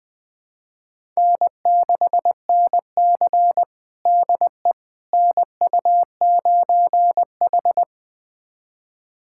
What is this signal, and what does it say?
Morse code, heard as a keyed tone, about 21 wpm, 700 Hz. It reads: N6NC DE NU9H
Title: USA Callsign DE Callsign - 4 Characters - 20wpm